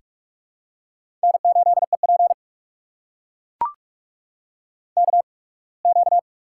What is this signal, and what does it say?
Morse code, heard as a keyed tone, about 44 wpm, spaced as for 27 wpm, 690 Hz. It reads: N8EP KQ
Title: USA Call Signs - 4 Characters - 8x Character Spacing - 45wpm